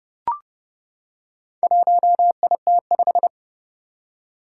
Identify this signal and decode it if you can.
Morse code, heard as a keyed tone, about 30 wpm, 700 Hz. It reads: 1IT5
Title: Four Challenging Letter-Number Combinations 30wpm